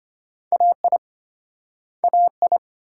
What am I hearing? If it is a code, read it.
Morse code, heard as a keyed tone, about 35 wpm, 700 Hz. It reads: AI AI